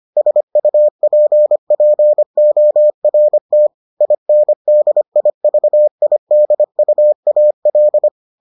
Morse code, heard as a keyed tone, about 25 wpm, 600 Hz. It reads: SUPPORT INDIVIDUAL